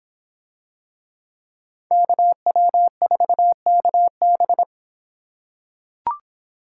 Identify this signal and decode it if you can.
Morse code, heard as a keyed tone, about 25 wpm, 700 Hz. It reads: KW4KB